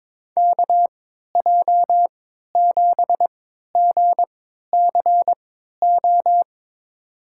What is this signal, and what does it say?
Morse code, heard as a keyed tone, about 22 wpm, spaced as for 13 wpm, 700 Hz. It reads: KJ7GCO
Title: USA Call Signs - 3x Character Spacing - 22wpm